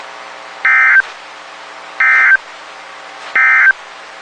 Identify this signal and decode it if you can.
Emergency Alert System (SAME) tones, heard as harsh buzzing bursts. It is heard from 0.5 to 3.8 s.
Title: WXK91 NOAA Weather Radio